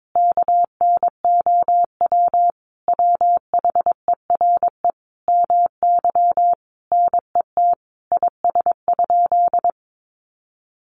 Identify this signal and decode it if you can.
Morse code, heard as a keyed tone, about 22 wpm, 700 Hz. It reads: KNOW WHERE MY NET IS?
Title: Sentences from Top 1400 Words 22wpm